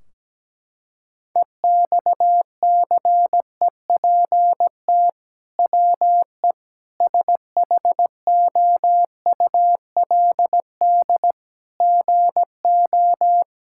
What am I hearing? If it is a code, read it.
Morse code, heard as a keyed tone, about 17 wpm, 700 Hz. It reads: EXCEPT WE SHOULD GO
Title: Sentences from Top 1600 Words 17wpm